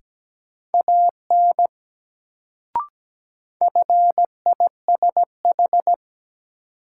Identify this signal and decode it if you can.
Morse code, heard as a keyed tone, about 17 wpm, 700 Hz. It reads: AN FISH
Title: Sentences from Top 300 Words - New Word Drill - 17wpm